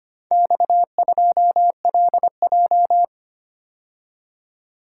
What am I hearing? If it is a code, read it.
Morse code, heard as a keyed tone, about 25 wpm, 700 Hz. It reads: X2LJ
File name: practice / 25wpm / Four Letter-Number Combinations - 25wpm